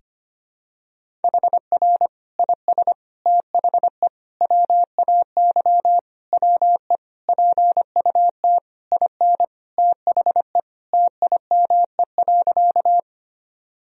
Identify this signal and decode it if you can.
Morse code, heard as a keyed tone, about 25 wpm, 700 Hz. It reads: HR IS THE WAY WE PUT IN THE TIME.
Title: Sentences from Top 200 Words and Abbreviations 25wpm